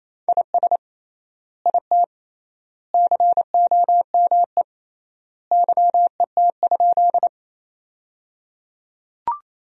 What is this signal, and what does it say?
Morse code, heard as a keyed tone, about 28 wpm, 700 Hz. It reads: IS IT COME YET?
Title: 3x Word-Spacing - Sentences from Top 500 Words - 28wpm